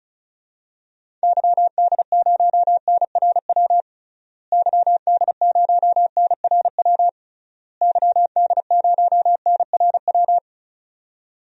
Morse code, heard as a keyed tone, about 35 wpm, 700 Hz. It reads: YD0NRW YD0NRW YD0NRW
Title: DXCC-100 Call Signs - 6 Characters - Repeated 3x - 35wpm